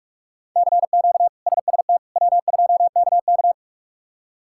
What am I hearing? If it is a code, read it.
Morse code, heard as a keyed tone, about 46 wpm, 700 Hz. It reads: CQ SST W2KK
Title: K1USN SST QSO Practice - 45wpm